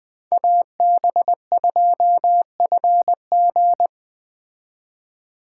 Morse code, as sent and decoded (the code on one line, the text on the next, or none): .- -... ..--- ..-. --.
AB2FG